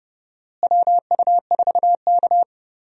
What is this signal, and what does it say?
Morse code, heard as a keyed tone, about 30 wpm, 700 Hz. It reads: WU4K
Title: USA Call Signs - 4 Characters - Repeated 3x - 30wpm